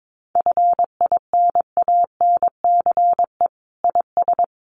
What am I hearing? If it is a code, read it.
Morse code, heard as a keyed tone, about 22 wpm, 700 Hz. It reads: FINANCE IS
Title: Sentences from Top 1500 Words 22wpm